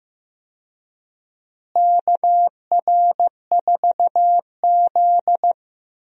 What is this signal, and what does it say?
Morse code, heard as a keyed tone, about 15 wpm, 700 Hz. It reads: KR4Z
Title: USA Call Signs - 4 Characters - Repeated 3x - 15wpm